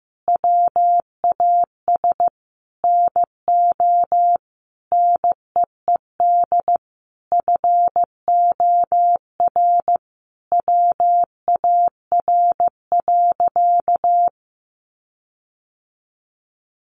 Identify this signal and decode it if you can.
Morse code, heard as a keyed tone, about 15 wpm, 700 Hz. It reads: WAS NO NEED FOR WAR.
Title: Sentences from Top 1300 Words 15wpm